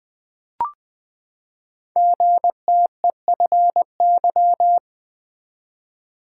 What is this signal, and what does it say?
Morse code, heard as a keyed tone, about 20 wpm, 700 Hz. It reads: GTEFY
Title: Code Groups - 5 Letter:Numbers - 20wpm